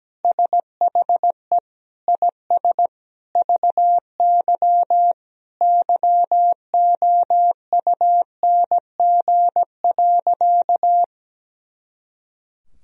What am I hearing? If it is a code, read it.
Morse code, heard as a keyed tone, about 17 wpm, 700 Hz. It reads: SHE IS VY YOUNG.
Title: Sentences from Top 400 Words and Abbreviations 17wpm